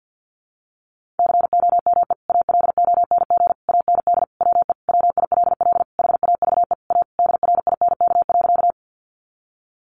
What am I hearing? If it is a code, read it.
Morse code, heard as a keyed tone, about 50 wpm, 700 Hz. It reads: COME ALONG AND WE WILL HAVE A DRINK.
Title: Sentences from Top 1500 Words 50wpm